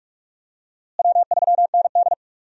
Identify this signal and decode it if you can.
Morse code, heard as a keyed tone, about 43 wpm, 700 Hz. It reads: W3ND
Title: USA Call Signs - 4 Characters - Repeated 3x - 45wpm